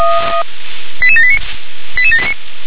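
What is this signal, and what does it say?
Five-tone selective call (ZVEI), heard as quick runs of five tones, tones of about 70 ms. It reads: 88789 88789
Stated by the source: ZVEI five-tone selective call received by SDR.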